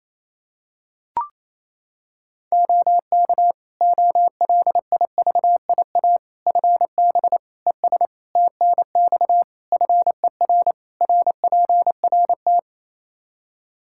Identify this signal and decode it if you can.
Morse code, heard as a keyed tone, about 28 wpm, 700 Hz. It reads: OK OLIVIA FB ES TNX FER RPRT
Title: Practice QSO - Exchange 2 - Info - 28wpm